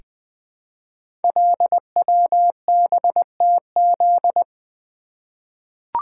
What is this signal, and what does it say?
Morse code, heard as a keyed tone, about 20 wpm, 700 Hz. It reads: LWBTZ
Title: Code Groups - 5 Letter:Numbers - 20wpm